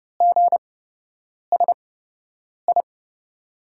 Morse code, heard as a keyed tone, about 31 wpm, spaced as for 8 wpm, 700 Hz. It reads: GSI
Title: USA Call Signs - 5 Characters - 8x Character Spacing - 30wpm